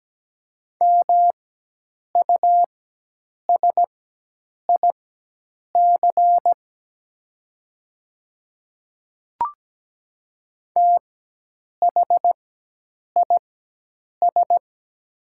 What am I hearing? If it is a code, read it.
Morse code, heard as a keyed tone, about 17 wpm, spaced as for 8 wpm, 700 Hz. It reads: MUSIC THIS
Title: 4x Character Spacing - Top 300 Words - 17wpm